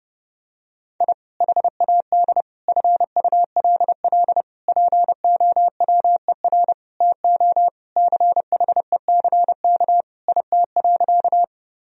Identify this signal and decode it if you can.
Morse code, heard as a keyed tone, about 30 wpm, 700 Hz. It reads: I HAD FULL POWER TO CHECK IT.
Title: Sentences from Top 1000 Words - Review Part 3 - 30wpm